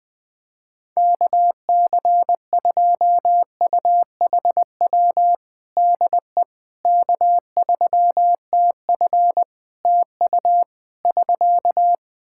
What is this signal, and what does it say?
Morse code, heard as a keyed tone, about 20 wpm, 700 Hz. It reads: KC2UHW DE K3TF TU <SK>
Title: Practice QSO - Exchange 4 - Ending - 20wpm